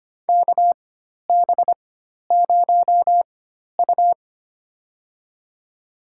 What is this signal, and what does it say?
Morse code, heard as a keyed tone, about 25 wpm, spaced as for 12 wpm, 700 Hz. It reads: KB0U
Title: USA Call Signs - 4 Characters - 4x Character Spacing - 25wpm